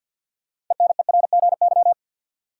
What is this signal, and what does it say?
Morse code, heard as a keyed tone, about 52 wpm, 700 Hz. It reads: ENERGY
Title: Sentences from Top 900 Words - New Word Drill - 50wpm